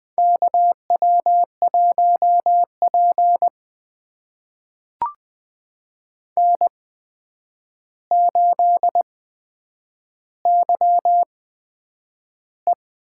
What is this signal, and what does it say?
Morse code, heard as a keyed tone, about 20 wpm, spaced as for 9 wpm, 700 Hz. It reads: KW1P N8YE
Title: USA Call Signs - 4 Characters - 8x Character Spacing - 20wpm